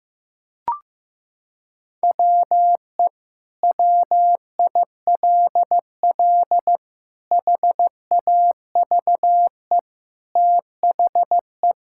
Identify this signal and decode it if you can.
Morse code, heard as a keyed tone, about 16 wpm, 700 Hz. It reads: WE WILL HAVE THE
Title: Sentences from Top 1000 Words - Review Part 6 - 15wpm